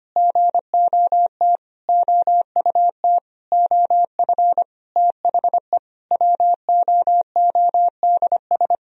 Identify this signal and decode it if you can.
Morse code, heard as a keyed tone, about 25 wpm, 700 Hz. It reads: GOT OUT OF THE WOODS
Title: Sentences from Top 1000 Words - Review Part 2 - 25wpm